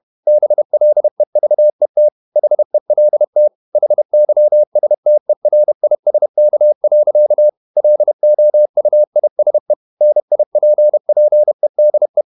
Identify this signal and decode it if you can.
Morse code, heard as a keyed tone, about 31 wpm, 600 Hz. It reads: BLEVET HELT HYSTERISK. LOUISE NIPPEDE
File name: Ferie QRQ 31wpm